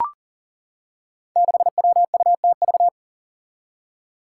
Morse code, heard as a keyed tone, about 40 wpm, 700 Hz. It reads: 6WUTV
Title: Five Challenging Letter-Number Combinations 40wpm